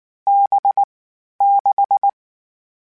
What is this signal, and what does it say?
Morse code, heard as a keyed tone, about 19 wpm, 800 Hz. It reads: B 6
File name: Difficult Combos / Difficult Combos-mp3s / NDB NDB6 @18